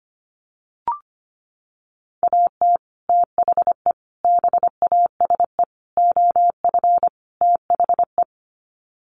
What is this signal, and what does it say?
Morse code, heard as a keyed tone, about 25 wpm, 700 Hz. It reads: AT THE BASE OF THE
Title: Sets of 5 Words 25wpm